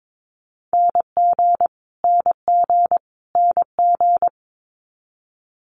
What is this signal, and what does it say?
Morse code, heard as a keyed tone, about 22 wpm, 700 Hz. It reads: NG NG NG
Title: Multi-letter Phonemes Repeated 3x 22wpm